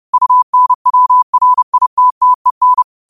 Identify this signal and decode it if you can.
Morse code, heard as a keyed tone, about 30 wpm, 1000 Hz. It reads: UNWRITTEN
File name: qotdfiles / 30wpm